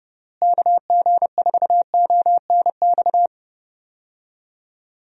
Morse code, heard as a keyed tone, about 30 wpm, 700 Hz. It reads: KG4ONX
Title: USA Call Signs - 6 Characters - Repeated 2x - 30wpm